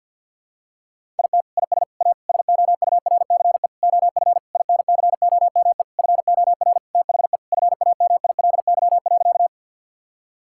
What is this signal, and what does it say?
Morse code, heard as a keyed tone, about 50 wpm, 700 Hz. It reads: IT IS A SOURCE OF INCOME FOR THE FAMILY.
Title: Sentences from Top 1000 Words - Review Part 5 - 50wpm